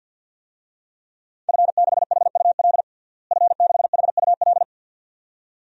Morse code, heard as a keyed tone, about 51 wpm, 700 Hz. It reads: F6HUL F6HUL
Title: DXCC-100 Call Signs - 5 Characters - Repeated 2x - 50wpm